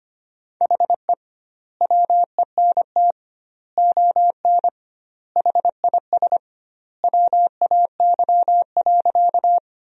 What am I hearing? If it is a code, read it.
Morse code, heard as a keyed tone, about 25 wpm, 700 Hz. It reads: HE WENT ON HIS WAY.
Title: 2x Word-Spacing - Sentences from Top 200 Words - 25wpm